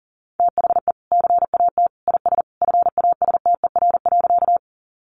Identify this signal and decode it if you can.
Morse code, heard as a keyed tone, about 40 wpm, 700 Hz. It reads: THE CAT IS FASTER.
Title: Additional Easy Phrases 40wpm